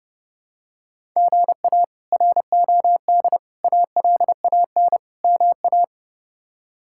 Morse code, heard as a keyed tone, about 30 wpm, 700 Hz. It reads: GA ROD ALAN MA
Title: K1USN SST QSO Practice - 30wpm